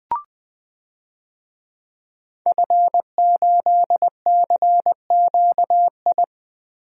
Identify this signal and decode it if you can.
Morse code, heard as a keyed tone, about 20 wpm, 700 Hz. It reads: F8CQI
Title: DXCC-100 Call Signs - 5 Characters - Repeated 3x - 20wpm